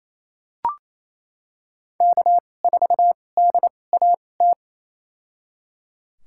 Morse code, heard as a keyed tone, about 28 wpm, spaced as for 20 wpm, 700 Hz. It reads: K4DAT
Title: USA Call Signs - 5 Characters - 2x Character Spacing - 28wpm